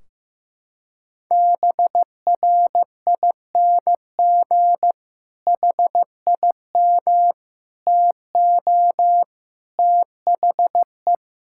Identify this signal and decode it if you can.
Morse code, heard as a keyed tone, about 15 wpm, 700 Hz. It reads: BRING HIM TO THE
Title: Sentences from Top 1600 Words 15wpm